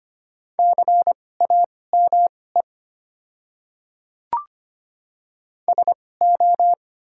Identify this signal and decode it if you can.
Morse code, heard as a keyed tone, about 25 wpm, spaced as for 18 wpm, 690 Hz. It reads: CAME SO
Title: 2x Character Spacing - Top 200 Words - 25wpm